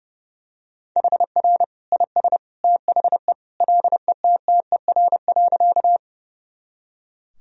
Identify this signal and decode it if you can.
Morse code, heard as a keyed tone, about 32 wpm, 700 Hz. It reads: HR IS THE LETTER.
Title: Sentences from Top 300 Words and Abbreviations 30wpm